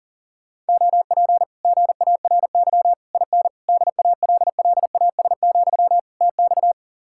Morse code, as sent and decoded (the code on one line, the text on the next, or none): --- .--. / --. .- .-. -.-- / .. -. / -.. .- .-.. .-.. .- ... --..-- / - -..-
OP GARY IN DALLAS, TX